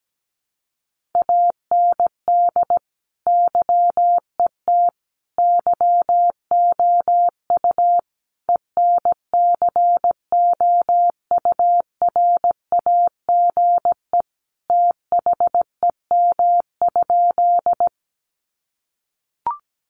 Morse code, heard as a keyed tone, about 17 wpm, 700 Hz. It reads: AND YET YOU ENCOURAGE THEM?
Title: Sentences from Top 1400 Words 17wpm